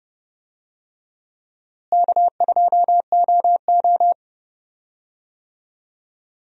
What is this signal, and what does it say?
Morse code, heard as a keyed tone, about 30 wpm, 700 Hz. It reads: K2OO
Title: USA Call Signs - 4 Characters - Repeated 2x - 30wpm